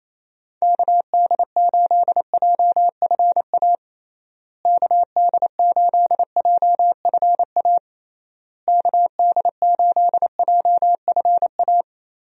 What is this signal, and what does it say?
Morse code, heard as a keyed tone, about 28 wpm, 700 Hz. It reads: KD8JFA KD8JFA KD8JFA
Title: USA Call Signs - 6 Characters - Repeated 3x - 28wpm